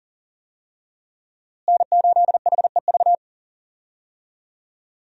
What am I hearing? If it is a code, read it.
Morse code, heard as a keyed tone, about 38 wpm, 700 Hz. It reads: N8HEV
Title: USA Call Signs - 5 Characters - Repeated 3x - 40wpm